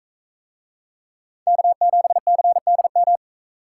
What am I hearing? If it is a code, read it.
Morse code, heard as a keyed tone, about 40 wpm, 700 Hz. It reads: K7CDM